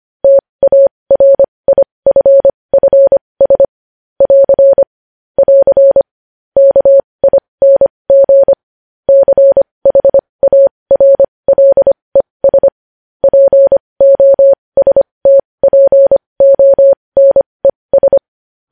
Morse code, heard as a keyed tone, about 25 wpm, spaced as for 20 wpm, 550 Hz. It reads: TARIFFS <AR> <AR> KING CHARLES POSTPONES
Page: news-headlines-in-morse-code-at-20-wpm